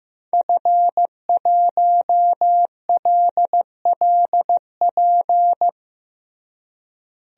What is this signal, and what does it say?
Morse code, heard as a keyed tone, about 15 wpm, 700 Hz. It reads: F1LLP
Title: DXCC-100 Call Signs - 5 Characters - Repeated 2x - 15wpm